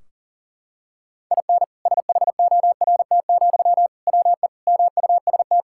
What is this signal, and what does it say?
Morse code, heard as a keyed tone, about 40 wpm, 700 Hz. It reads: IN SHORT, WE MUST
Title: Sentences from Top 1600 Words 40wpm